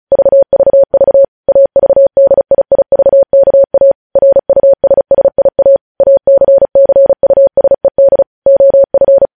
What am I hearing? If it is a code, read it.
Morse code, heard as a keyed tone, about 36 wpm, 550 Hz. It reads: VVV AVDIIVKA RUSSIA ACCUSED OF